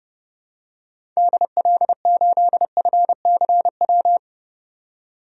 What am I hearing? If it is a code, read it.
Morse code, heard as a keyed tone, about 32 wpm, 700 Hz. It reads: DL8FCW
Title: DXCC-100 Call Signs - 6 Characters - 30wpm